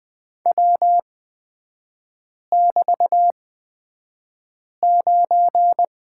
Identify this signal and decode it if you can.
Morse code, heard as a keyed tone, about 20 wpm, spaced as for 5 wpm, 700 Hz. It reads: W<BT>9
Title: 224 Lesson 36e - BT - Three Characters 20wpm